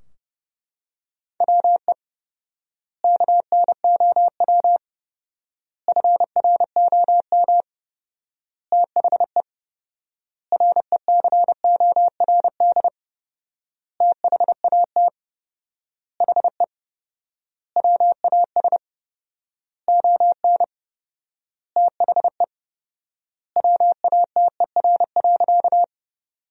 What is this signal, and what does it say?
Morse code, heard as a keyed tone, about 32 wpm, 700 Hz. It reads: WE KNOW FROM THE RECORD THAT HE WAS ON THE WATER.
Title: 4x Word-Spacing - Sentences from Top 500 Words - 30wpm